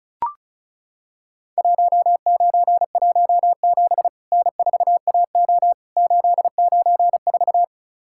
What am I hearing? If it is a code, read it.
Morse code, heard as a keyed tone, about 36 wpm, 700 Hz. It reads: 1917 N4AO 894
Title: Worked All Europe DX Contest - QTC Exchange - 35wpm